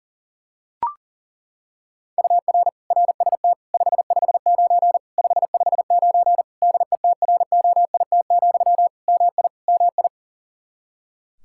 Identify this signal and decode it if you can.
Morse code, heard as a keyed tone, about 40 wpm, 700 Hz. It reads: UR RST 559 559 DETROIT, MI MI
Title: Practice QSO - Exchange 1 - Beginning - 40wpm